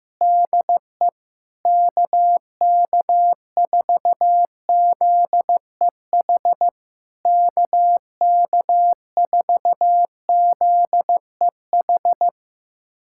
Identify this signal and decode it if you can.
Morse code, heard as a keyed tone, about 15 wpm, 700 Hz. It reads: DE KK4ZEH KK4ZEH
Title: SOTA QSO Practice 15wpm